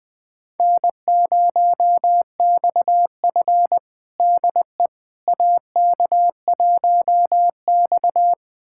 Morse code, heard as a keyed tone, about 20 wpm, 700 Hz. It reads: N0XF DE AK1X
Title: USA Callsign DE Callsign - 4 Characters - 20wpm